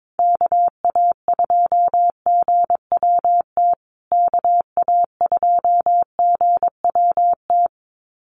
Morse code, heard as a keyed tone, about 22 wpm, 700 Hz. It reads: KA2GWT KA2GWT
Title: USA Call Signs Repeated 2x 22wpm